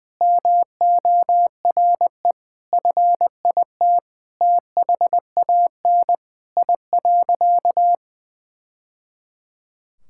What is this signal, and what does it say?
Morse code, heard as a keyed tone, about 20 wpm, 700 Hz. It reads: MORE FIT THAN I.
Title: Sentences from Top 1000 Words - Review Part 5 - 20wpm